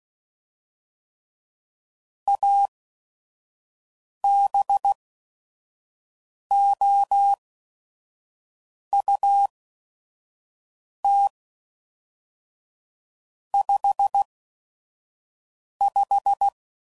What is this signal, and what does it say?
Morse code, heard as a keyed tone, about 16 wpm, spaced as for 5 wpm, 800 Hz. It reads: ABOUT 55
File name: CW Code Groups @5WPM / Weather Sets / Weather 4 24